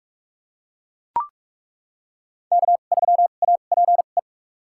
Morse code, heard as a keyed tone, about 42 wpm, spaced as for 31 wpm, 700 Hz. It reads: K3APE
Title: USA Call Signs - 5 Characters - 2x Character Spacing - 45wpm